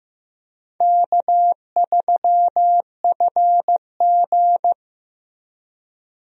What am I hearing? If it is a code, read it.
Morse code, heard as a keyed tone, about 15 wpm, 700 Hz. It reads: K3FG
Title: USA Call Signs - 4 Characters - Repeated 2x - 15wpm